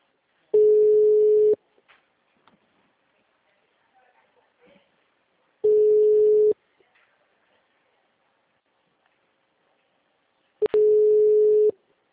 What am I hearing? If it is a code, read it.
Ringback tone (Europe, 425 Hz).